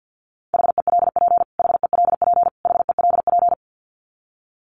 Morse code, heard as a keyed tone, about 50 wpm, 700 Hz. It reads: HELP HELP HELP
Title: Top 500 Words Repeated 50wpm